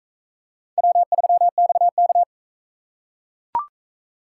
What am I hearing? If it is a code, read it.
Morse code, heard as a keyed tone, about 40 wpm, 700 Hz. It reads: W3XK